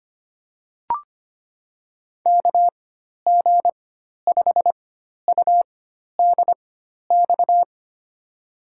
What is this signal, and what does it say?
Morse code, heard as a keyed tone, about 25 wpm, spaced as for 12 wpm, 700 Hz. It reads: KG5UDX
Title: USA Call Signs - 4x Character Spacing - 25wpm